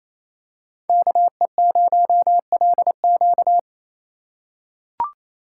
Morse code, heard as a keyed tone, about 28 wpm, 700 Hz. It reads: KE0LQ
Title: USA Call Signs - 5 Characters - 4x Character Spacing - 28wpm